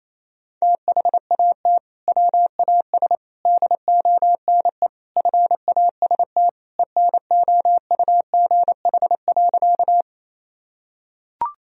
Morse code, heard as a keyed tone, about 28 wpm, 700 Hz. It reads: THAT WAS DONE FAST ENOUGH.
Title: Sentences from Top 1000 Words - Review Part 2 - 28wpm